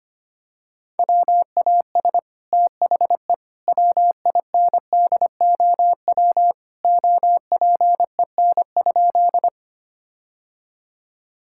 Morse code, heard as a keyed tone, about 25 wpm, 700 Hz. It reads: WAS THE WINDOW OPEN?
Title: Sentences from Top 1600 Words 25wpm